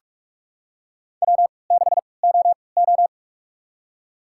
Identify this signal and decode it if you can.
Morse code, heard as a keyed tone, about 45 wpm, spaced as for 25 wpm, 700 Hz. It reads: W6OO
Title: USA Call Signs - 4 Characters - 3x Character Spacing - 45wpm